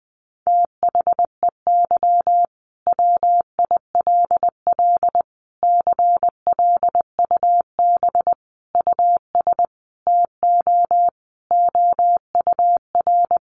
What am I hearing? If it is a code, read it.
Morse code, heard as a keyed tone, about 20 wpm, 700 Hz. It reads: THEY WILL CLUB US TO OUR
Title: Sentences from Top 1200 Words 20wpm